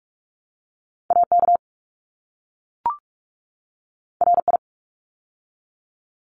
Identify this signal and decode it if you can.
Morse code, heard as a keyed tone, about 44 wpm, 700 Hz. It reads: AK RI
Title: ARRL Sections 45wpm